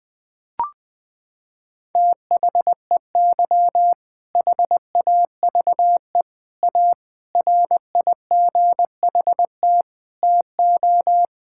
Morse code, heard as a keyed tone, about 21 wpm, 700 Hz. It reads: THEY HAVE A RIGHT TO
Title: Sentences from Top 1000 Words - Review Part 5 - 20wpm